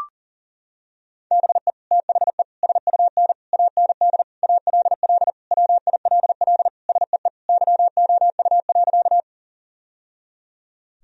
Morse code, heard as a keyed tone, about 40 wpm, 700 Hz. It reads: BE THE SUN AND ALL WILL SEE YOU.
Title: Sentences from Top 1000 Words - Review Part 1 - 40wpm